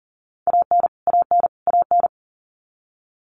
Morse code, heard as a keyed tone, about 40 wpm, 700 Hz. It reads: AN AN AN
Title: Top 300 Words Repeated 40wpm